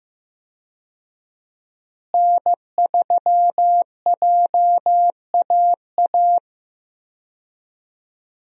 Morse code, heard as a keyed tone, about 15 wpm, 700 Hz. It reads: N3JAA
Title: USA Call Signs - 5 Characters - Repeated 2x - 15wpm